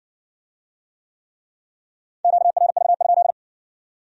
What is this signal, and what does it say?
Morse code, heard as a keyed tone, about 66 wpm, 700 Hz. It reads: QRV?